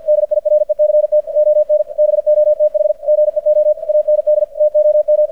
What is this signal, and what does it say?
Morse code, heard as a keyed tone, about 50 wpm, 600 Hz. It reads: DIRECT ON BOTH GEO AND TOM
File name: archived / CRSnet